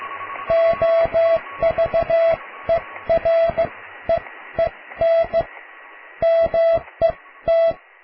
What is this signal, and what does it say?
Morse code, heard as a keyed tone, about 15 wpm, spaced as for 13 wpm, 660 Hz. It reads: OVEREEN MET
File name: cw / tekst(a)+QRM